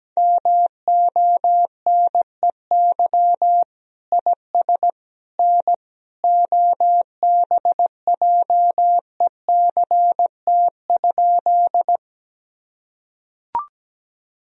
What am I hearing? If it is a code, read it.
Morse code, heard as a keyed tone, about 17 wpm, 700 Hz. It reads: MONEY IS N OBJECT?
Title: Sentences from Top 500 Words and Abbreviations 17wpm